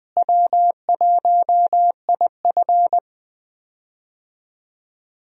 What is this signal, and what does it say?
Morse code, heard as a keyed tone, about 20 wpm, 700 Hz. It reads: W1IF
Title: USA Call Signs - 4 Characters - Repeated 3x - 20wpm